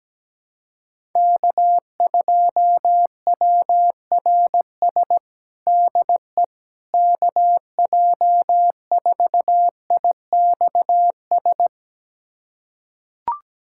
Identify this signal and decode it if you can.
Morse code, heard as a keyed tone, about 17 wpm, 700 Hz. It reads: K2WRS DE KJ4IXS
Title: USA Callsign DE Callsign 17wpm